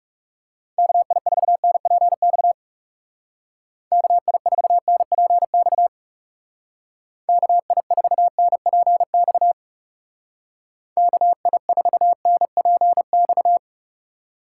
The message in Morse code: KI4NPX KI4NPX KI4NPX KI4NPX

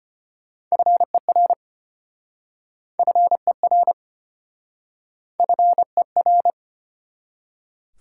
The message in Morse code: FER FER FER

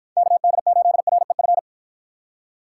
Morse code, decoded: KD8REF

